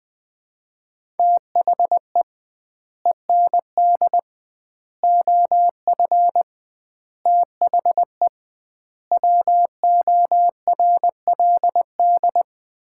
Morse code, heard as a keyed tone, about 21 wpm, 700 Hz. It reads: THE END OF THE WORLD
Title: 2x Word-Spacing - Sets of 5 Words - 20wpm